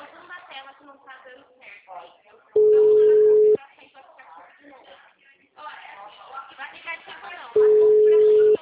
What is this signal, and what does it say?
Ringback tone (Europe, 425 Hz), with speech.